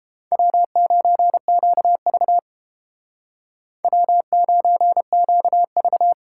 Morse code, heard as a keyed tone, about 30 wpm, 700 Hz. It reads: W9QV W9QV